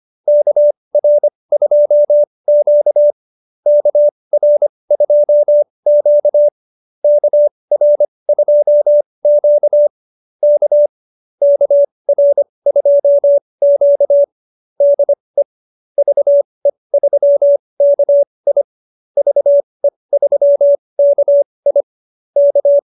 Morse code, heard as a keyed tone, about 25 wpm, spaced as for 20 wpm, 590 Hz. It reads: KR2Q KR2Q KR2Q K KR2Q DE VE3KI VE3KI K